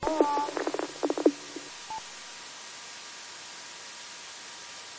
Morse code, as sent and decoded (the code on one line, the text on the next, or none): none